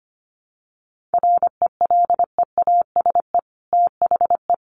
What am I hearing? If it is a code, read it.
Morse code, heard as a keyed tone, about 25 wpm, 690 Hz. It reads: RELEASE THE
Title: Sentences from Top 1100 Words 25wpm